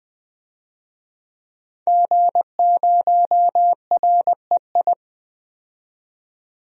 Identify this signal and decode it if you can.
Morse code, heard as a keyed tone, about 20 wpm, 700 Hz. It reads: G0REI
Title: DXCC-100 Call Signs - 5 Characters - 20wpm